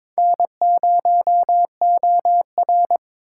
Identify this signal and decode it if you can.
Morse code, heard as a keyed tone, about 23 wpm, 700 Hz. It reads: N0OR
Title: USA Call Signs - 4 Characters - 4x Character Spacing - 22wpm